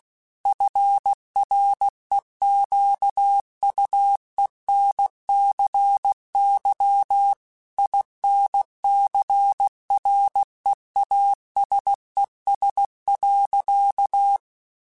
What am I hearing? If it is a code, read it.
Morse code, heard as a keyed tone, about 16 wpm, 800 Hz. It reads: FREQUENCY INCREASES.
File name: Technical Subjects / Technical Subjects-mp3s / XC @15